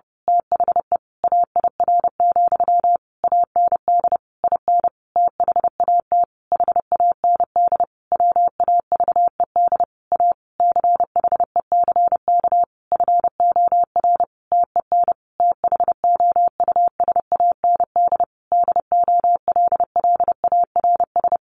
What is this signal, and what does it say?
Morse code, heard as a keyed tone, about 30 wpm, 700 Hz. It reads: THE AIR, AND IN THAT HAND WAVED A CHECK FOR TEN THOUSAND DOLLARS
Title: A Sound Of Thunder 30wpm